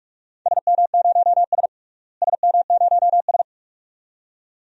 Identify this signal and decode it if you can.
Morse code, heard as a keyed tone, about 44 wpm, 700 Hz. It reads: SM0S SM0S
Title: DXCC-100 Call Signs - 4 Characters - Repeated 2x - 45wpm